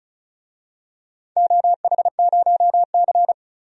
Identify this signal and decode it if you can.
Morse code, heard as a keyed tone, about 36 wpm, 700 Hz. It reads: OH0C
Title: DXCC-100 Call Signs - 4 Characters - 35wpm